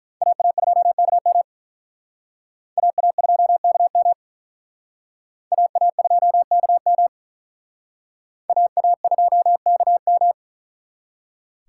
Morse code, heard as a keyed tone, about 41 wpm, 700 Hz. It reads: AA2KM AA2KM AA2KM AA2KM
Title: Speed Racing - USA Call Signs 35wpm